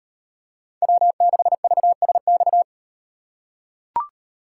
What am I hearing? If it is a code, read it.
Morse code, heard as a keyed tone, about 38 wpm, 700 Hz. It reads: W6VSX